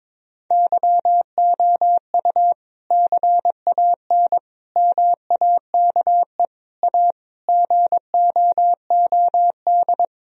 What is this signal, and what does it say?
Morse code, heard as a keyed tone, about 22 wpm, 700 Hz. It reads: YOU CAN MAKE A GOOD
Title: Sentences from Top 1000 Words - Review Part 4 - 22wpm